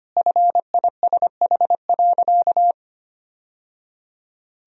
Morse code, heard as a keyed tone, about 25 wpm, 700 Hz. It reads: FISH.
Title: Sentences from Top 1000 Words - Review Part 2 - 25wpm